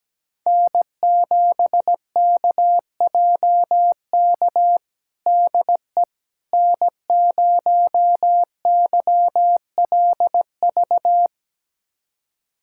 Morse code, heard as a keyed tone, about 17 wpm, 700 Hz. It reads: N7KJK DE N0YLV
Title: USA Callsign DE Callsign - 5 Characters - 17wpm